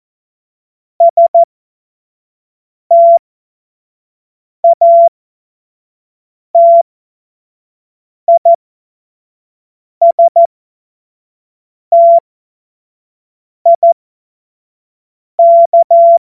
Morse code, heard as a keyed tone, about 14 wpm, spaced as for 5 wpm, 680 Hz. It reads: STATISTIK
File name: Morsekurs / Witze / Witz 26. April 2022 25 F70